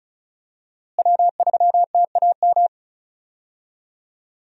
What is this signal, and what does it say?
Morse code, heard as a keyed tone, about 34 wpm, 700 Hz. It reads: W3TAM
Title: USA Call Signs - 5 Characters - Repeated 3x - 35wpm